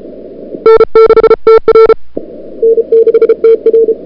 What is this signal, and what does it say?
Morse code, heard as a keyed tone, about 34 wpm, 440 Hz. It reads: N6TR N6TR
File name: CWSprint / February 2007 CW Sprint / 01-2007 CW Sprint Hour1